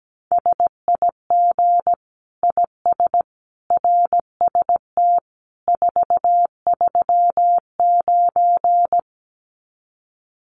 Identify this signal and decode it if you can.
Morse code, heard as a keyed tone, about 17 wpm, 700 Hz. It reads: SIG IS RST 439
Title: Common QSO Phrases 17wpm